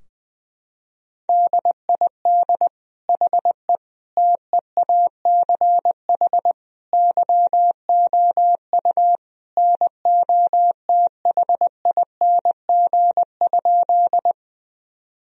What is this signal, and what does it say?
Morse code, heard as a keyed tone, about 20 wpm, 700 Hz. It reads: DID HE TEACH YOU NOTHING?
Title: Sentences from Top 1000 Words - Review Part 2 - 20wpm